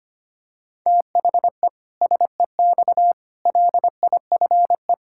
Morse code, heard as a keyed tone, about 25 wpm, 700 Hz. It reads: THE SEX LIFE
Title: Pangram Challenge v2 - 25wpm